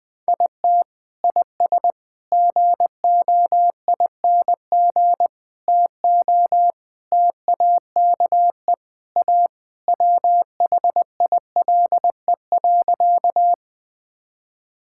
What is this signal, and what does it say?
Morse code, heard as a keyed tone, about 20 wpm, 700 Hz. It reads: IT IS GOING TO TAKE A WHILE.
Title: Sentences with 4-Word Phrases - 20wpm